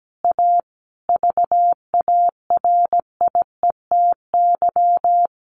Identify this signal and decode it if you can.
Morse code, heard as a keyed tone, about 17 wpm, 700 Hz. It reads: A VARIETY
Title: Sets of 5 Words Encore 17wpm